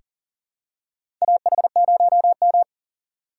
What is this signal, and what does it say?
Morse code, heard as a keyed tone, about 40 wpm, 700 Hz. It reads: AH0M
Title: USA Call Signs - 4 Characters - 6x Character Spacing - 40wpm